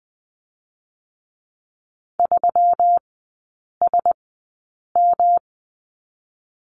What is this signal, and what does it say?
Morse code, heard as a keyed tone, about 20 wpm, 700 Hz. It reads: 3 S M
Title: 121 Lesson 20e - Number 3 - Three Characters 20wpm